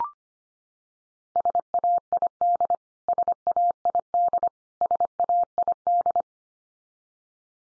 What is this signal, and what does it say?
Morse code, heard as a keyed tone, about 25 wpm, 700 Hz. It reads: SAID SAID SAID
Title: Top 100 Words 25wpm x3